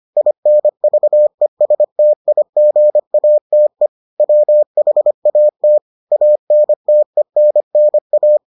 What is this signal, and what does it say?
Morse code, heard as a keyed tone, about 25 wpm, 600 Hz. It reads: INVESTIGATE WHAT ANTENNA